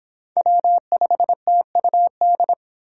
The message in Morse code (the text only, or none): W5TUD